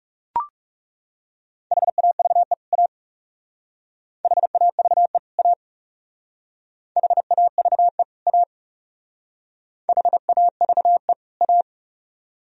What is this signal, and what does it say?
Morse code, heard as a keyed tone, about 36 wpm, 700 Hz. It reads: HAVE A HAVE A HAVE A HAVE A
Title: Speed-Racing - Sets of 2 Words 30wpm